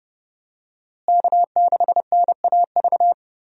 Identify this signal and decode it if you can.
Morse code, heard as a keyed tone, about 30 wpm, 700 Hz. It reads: K6NAV
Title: USA Call Signs - 5 Characters - Repeated 3x - 30wpm